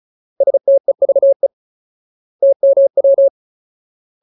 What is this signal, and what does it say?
Morse code, heard as a keyed tone, about 36 wpm, 550 Hz. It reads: STEVE TMW
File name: Ultimate Word List / 35WPM Ultimate